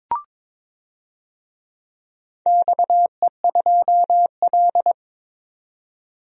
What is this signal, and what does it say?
Morse code, heard as a keyed tone, about 22 wpm, 700 Hz. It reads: XE2L
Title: DXCC-100 Call Signs - 4 Characters - 22wpm